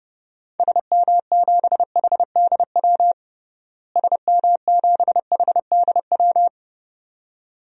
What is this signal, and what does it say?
Morse code, heard as a keyed tone, about 31 wpm, 700 Hz. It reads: SM7HDW SM7HDW
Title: DXCC-100 Call Signs - 6 Characters - Repeated 2x - 30wpm